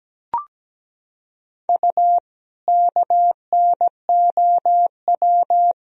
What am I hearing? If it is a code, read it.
Morse code, heard as a keyed tone, about 17 wpm, 700 Hz. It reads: U KNOW
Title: Sentences from Top 100 Words and Abbreviations 17wpm